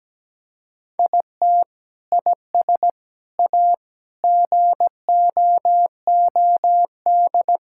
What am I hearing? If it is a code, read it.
Morse code, heard as a keyed tone, about 17 wpm, 700 Hz. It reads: IT IS A GOOD
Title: Sentences from Top 1000 Words - Review Part 5 - 17wpm